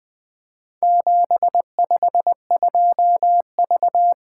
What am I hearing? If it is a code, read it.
Morse code, heard as a keyed tone, about 20 wpm, 700 Hz. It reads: 752V
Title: Four Letter-Number Combinations - 20wpm